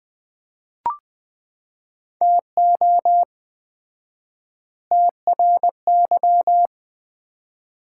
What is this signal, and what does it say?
Morse code, heard as a keyed tone, about 21 wpm, 700 Hz. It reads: TO TRY
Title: 4x Word-Spacing - Sets of 3 Words - 20wpm